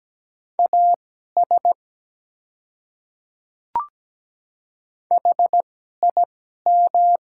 Morse code, heard as a keyed tone, about 17 wpm, spaced as for 12 wpm, 700 Hz. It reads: AS HIM
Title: 2x Character Spacing - Top 100 Words - 17wpm